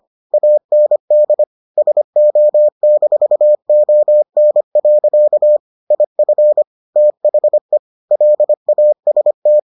Morse code, heard as a keyed tone, about 25 wpm, 600 Hz. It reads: AND SO-ON. IF THE LAST